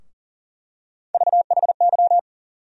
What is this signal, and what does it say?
Morse code, heard as a keyed tone, about 40 wpm, 700 Hz. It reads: VHY
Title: USA Call Sign Postfixes - 3 Letters 40wpm